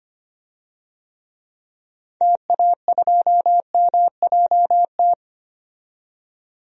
Morse code, heard as a keyed tone, about 25 wpm, 700 Hz. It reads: TA2MJT
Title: DXCC-100 Call Signs - 6 Characters - 25wpm